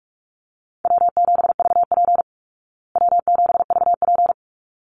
Morse code, heard as a keyed tone, about 44 wpm, 700 Hz. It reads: W7VP W7VP
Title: Call Signs Repeated 3x - 45wpm